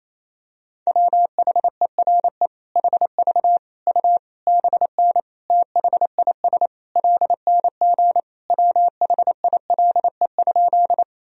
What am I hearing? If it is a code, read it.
Morse code, heard as a keyed tone, about 28 wpm, 700 Hz. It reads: WHERE HV U BN THIS LNG WHILE?
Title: Sentences from Top 300 Words and Abbreviations 28wpm